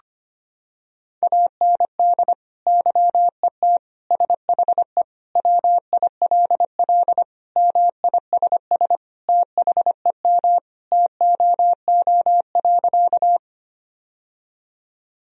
Morse code, heard as a keyed tone, about 25 wpm, 700 Hz. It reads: AND YET SHE WILL MISS THEM TOO.
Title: Sentences from Top 1000 Words - Review Part 3 - 25wpm